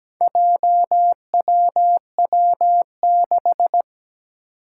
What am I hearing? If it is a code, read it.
Morse code, heard as a keyed tone, about 17 wpm, 700 Hz. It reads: JWW6
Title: Four Challenging Letter-Number Combinations 17wpm